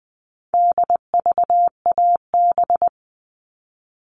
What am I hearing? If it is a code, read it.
Morse code, heard as a keyed tone, about 20 wpm, 700 Hz. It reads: DVAB
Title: All Cut Numbers - 4-Digit 20wpm